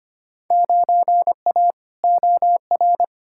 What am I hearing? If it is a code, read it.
Morse code, heard as a keyed tone, about 25 wpm, 700 Hz. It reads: 9A OR
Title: Field Day Exchanges Encore - 25wpm